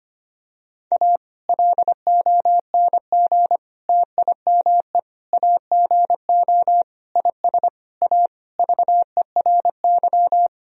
Morse code, heard as a keyed tone, about 25 wpm, 700 Hz. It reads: A LONG TIME AGO IS A VERY
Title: Sentences from Top 1000 Words - Review Part 3 - 25wpm